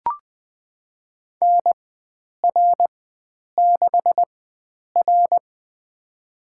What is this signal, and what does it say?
Morse code, heard as a keyed tone, about 21 wpm, spaced as for 9 wpm, 700 Hz. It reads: NR6R
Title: USA Call Signs - 4x Character Spacing - 20wpm